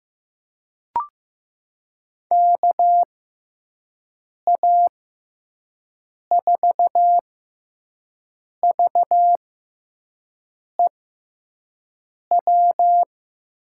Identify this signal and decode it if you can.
Morse code, heard as a keyed tone, about 15 wpm, spaced as for 5 wpm, 700 Hz. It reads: KA4VEW
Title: USA Call Signs - 6 Characters - 6x Character Spacing - 15wpm